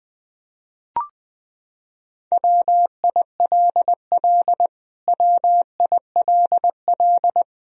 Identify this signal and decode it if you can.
Morse code, heard as a keyed tone, about 20 wpm, 700 Hz. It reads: WILL WILL
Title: Top 100 Words Repeated 20wpm